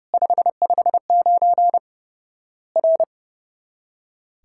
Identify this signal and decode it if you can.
Morse code, heard as a keyed tone, about 30 wpm, 700 Hz. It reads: 559 R
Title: Worked All Europe DX Contest - QTC Exchange - 30wpm